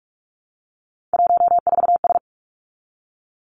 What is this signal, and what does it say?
Morse code, heard as a keyed tone, about 46 wpm, 700 Hz. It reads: 14S
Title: Three Challenging Letter-Number Combinations 45wpm